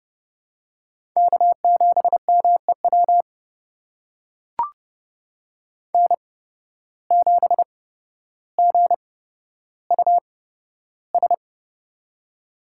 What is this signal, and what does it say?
Morse code, heard as a keyed tone, about 32 wpm, spaced as for 13 wpm, 700 Hz. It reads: K7MEW N7GUS
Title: USA Call Signs - 5 Characters - 8x Character Spacing - 30wpm